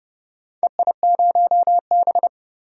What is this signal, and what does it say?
Morse code, heard as a keyed tone, about 30 wpm, 700 Hz. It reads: EI0B
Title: DXCC-100 Call Signs - 4 Characters - 30wpm